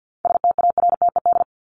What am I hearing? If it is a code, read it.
Morse code, heard as a keyed tone, about 50 wpm, 700 Hz. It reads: STARTED
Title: ED Suffix - Top 100 Words with Base Words 50wpm